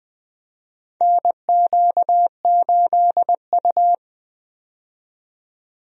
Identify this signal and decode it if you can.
Morse code, heard as a keyed tone, about 20 wpm, 700 Hz. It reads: NQ8U
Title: USA Call Signs - 4 Characters - 20wpm